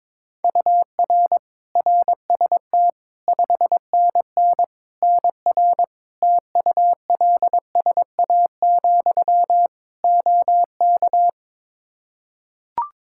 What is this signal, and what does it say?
Morse code, heard as a keyed tone, about 22 wpm, 700 Hz. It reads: UR RST 5NN NR TULSA, OK
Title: Practice QSO - Exchange 1 - Beginning - 22wpm